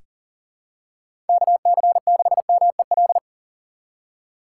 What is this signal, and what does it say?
Morse code, heard as a keyed tone, about 40 wpm, 700 Hz. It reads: KC6MEL
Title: USA Call Signs - 4x Character Spacing - 40wpm